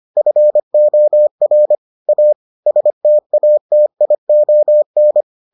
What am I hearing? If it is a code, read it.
Morse code, heard as a keyed tone, about 25 wpm, 600 Hz. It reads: FOR A STATION